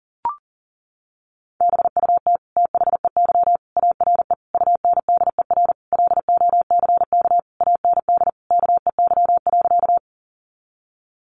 Morse code, heard as a keyed tone, about 40 wpm, 700 Hz. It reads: BUT THEY ARE UNDER LOCK AND KEY.
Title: Sentences from Top 1300 Words 40wpm